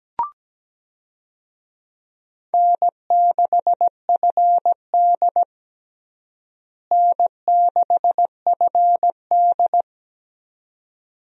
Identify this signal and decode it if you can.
Morse code, heard as a keyed tone, about 17 wpm, 700 Hz. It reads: N6FD N6FD
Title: USA Call Signs - 4 Characters - Repeated 2x - 17wpm